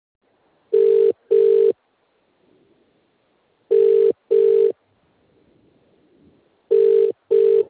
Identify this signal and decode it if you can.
Ringback tone (UK), 400 + 450 Hz.